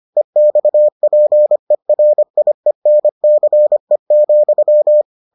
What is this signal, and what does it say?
Morse code, heard as a keyed tone, about 25 wpm, 600 Hz. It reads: EXPERIENCE,